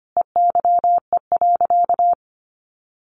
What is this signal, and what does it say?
Morse code, heard as a keyed tone, about 25 wpm, 700 Hz. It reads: EYE.
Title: Sentences from Top 1400 Words 25wpm